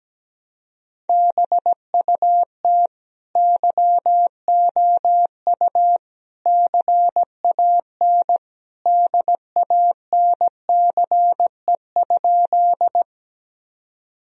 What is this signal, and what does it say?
Morse code, heard as a keyed tone, about 17 wpm, 700 Hz. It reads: BUT YOU CAN DANCE?
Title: Sentences from Top 1000 Words - Review Part 4 - 17wpm